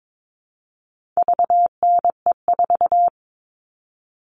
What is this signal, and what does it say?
Morse code, heard as a keyed tone, about 22 wpm, 700 Hz. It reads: VNE4
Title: All Cut Numbers - 4-Digit 22wpm